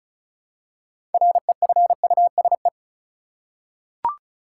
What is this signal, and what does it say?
Morse code, heard as a keyed tone, about 34 wpm, 700 Hz. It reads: REFUSE
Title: Six Letter Words - 35wpm